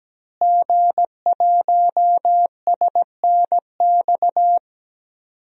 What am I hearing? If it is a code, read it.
Morse code, heard as a keyed tone, about 17 wpm, 700 Hz. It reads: G1SNX
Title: DXCC-100 Call Signs - 5 Characters - Repeated 3x - 17wpm